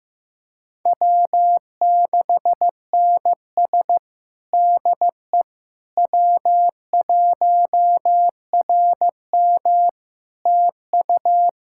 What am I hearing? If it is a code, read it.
Morse code, heard as a keyed tone, about 15 wpm, 700 Hz. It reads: W6NS DE W1RM TU
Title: Practice QSO - Exchange 4 - Ending - 15wpm